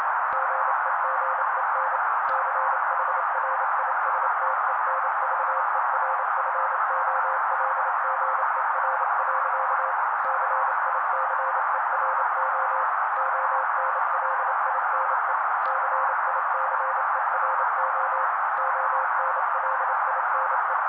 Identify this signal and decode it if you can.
Morse code, heard as a keyed tone, about 28 wpm, 530 Hz. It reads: GEGEN CHRISTENVERFOLGER. CICERO ONLINE CICERO ONLINE
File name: daily / 27wpm noise